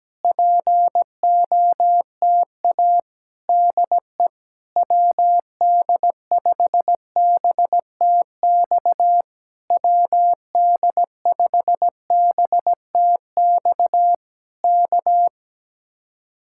Morse code, heard as a keyed tone, about 17 wpm, 700 Hz. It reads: POTA DE WD5BTX WD5BTX K
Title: Speed-Racing Common QSO Phrases - 17wpm